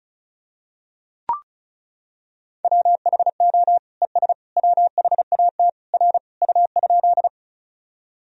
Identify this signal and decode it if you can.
Morse code, heard as a keyed tone, about 36 wpm, 700 Hz. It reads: WHO ES WHAT R U?
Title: Sentences from Top 100 Words and Abbreviations 35wpm